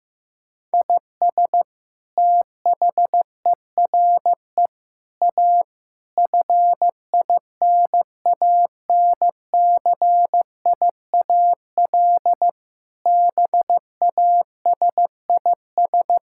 Morse code, heard as a keyed tone, about 16 wpm, 700 Hz. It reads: IS THERE A FINANCIAL BASIS
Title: Sentences from Top 1000 Words - Review Part 5 - 15wpm